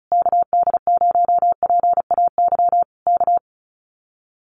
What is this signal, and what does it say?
Morse code, heard as a keyed tone, about 34 wpm, 700 Hz. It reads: KD0PAY K
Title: Common QSO Phrases Encore 35wpm